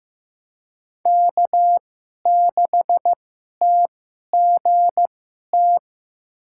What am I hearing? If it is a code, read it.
Morse code, heard as a keyed tone, about 15 wpm, spaced as for 11 wpm, 700 Hz. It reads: K6TGT
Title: USA Call Signs - 5 Characters - 2x Character Spacing - 15wpm